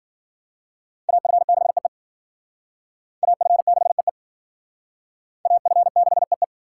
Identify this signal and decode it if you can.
Morse code, heard as a keyed tone, about 50 wpm, 700 Hz. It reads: AF6EE AF6EE AF6EE